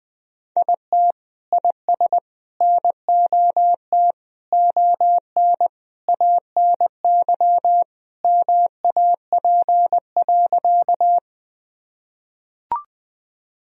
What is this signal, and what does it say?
Morse code, heard as a keyed tone, about 20 wpm, 700 Hz. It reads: IT IS NOT ON ANY MAP.
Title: Sentences from Top 1000 Words - Review Part 2 - 20wpm